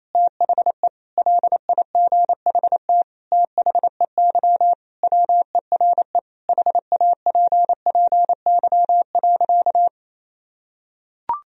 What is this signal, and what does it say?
Morse code, heard as a keyed tone, about 28 wpm, 700 Hz. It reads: THE LIGHT THEY WERE HAPPY.
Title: Sentences from Top 1000 Words - Review Part 1 - 28wpm